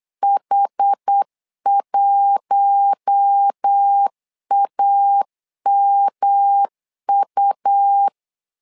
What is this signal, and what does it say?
Morse code, heard as a keyed tone, about 8 wpm, 800 Hz. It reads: H1AMU
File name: LEZIONI / 4 Q1 8wpm 3035